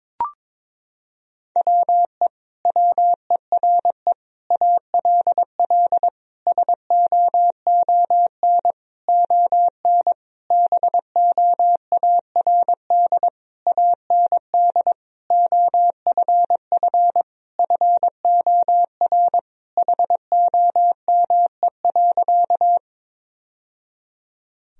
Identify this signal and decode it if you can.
Morse code, heard as a keyed tone, about 22 wpm, 700 Hz. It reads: WE WERE ALL SOON ON BOARD AND OFF FOR HOME.
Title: Sentences from Top 1000 Words - Review Part 4 - 22wpm